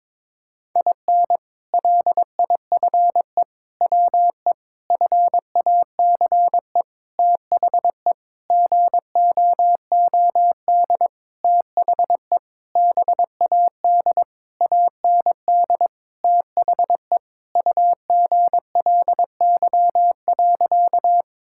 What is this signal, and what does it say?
Morse code, heard as a keyed tone, about 22 wpm, 700 Hz. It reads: IN LIFE WE FACE THE GOOD THE BAD AND THE UGLY.
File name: Sentences with 7-Word Phrases - 22wpm